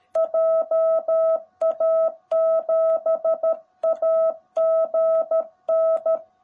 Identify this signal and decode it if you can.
Morse code, heard as a keyed tone, about 13 wpm, 650 Hz. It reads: JA7AGN
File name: jh8cbh / 2021cw / cw18b